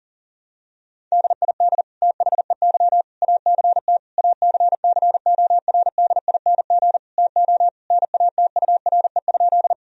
Morse code, heard as a keyed tone, about 40 wpm, 700 Hz. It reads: DID THEY ACT ACCORDING TO NATURE?
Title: Sentences from Top 1000 Words - Review Part 4 - 40wpm